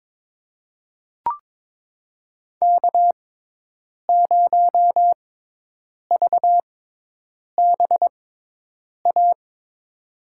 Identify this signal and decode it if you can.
Morse code, heard as a keyed tone, about 22 wpm, spaced as for 8 wpm, 700 Hz. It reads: K0VBA
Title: USA Call Signs - 5 Characters - 6x Character Spacing - 22wpm